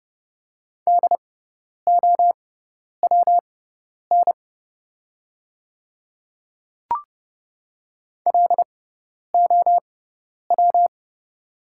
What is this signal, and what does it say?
Morse code, heard as a keyed tone, about 30 wpm, spaced as for 10 wpm, 700 Hz. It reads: DOWN LOW
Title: 6x Character Spacing - Top 300 Words - 30wpm